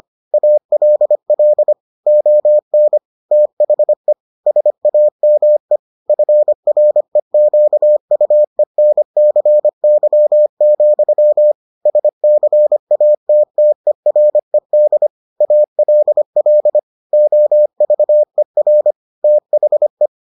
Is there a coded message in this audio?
Yes: Morse code from 0.3 to 20.1 s, heard as a keyed tone, about 25 wpm, 600 Hz. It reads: ALL ON THE SAME FREQUENCY, SCATTERED ALL OVER THE